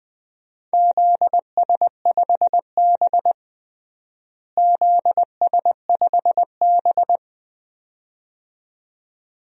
Morse code, heard as a keyed tone, about 20 wpm, 700 Hz. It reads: ZS5B ZS5B
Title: DXCC-100 Call Signs - 4 Characters - Repeated 2x - 20wpm